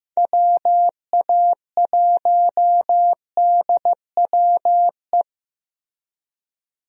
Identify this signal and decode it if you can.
Morse code, heard as a keyed tone, about 15 wpm, 700 Hz. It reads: WA1DWE
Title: USA Call Signs - 6 Characters - Repeated 3x - 15wpm